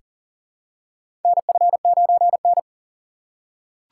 Morse code, heard as a keyed tone, about 40 wpm, 700 Hz. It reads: NF9N
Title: USA Call Signs - 4 Characters - 4x Character Spacing - 40wpm